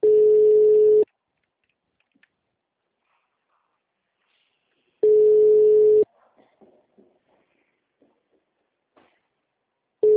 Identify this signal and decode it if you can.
Ringback tone (Europe, 425 Hz).